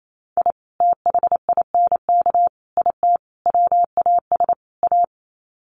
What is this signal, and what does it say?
Morse code, heard as a keyed tone, about 28 wpm, 700 Hz. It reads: I THINK IT WAS A
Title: Sets of 5 Words 28wpm